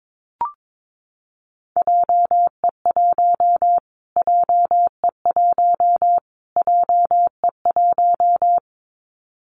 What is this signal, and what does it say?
Morse code, heard as a keyed tone, about 22 wpm, 700 Hz. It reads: JE1 JE1 JE1
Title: Callsign Prefixes for Top 25 DXCC - Repeated 3x - 22wpm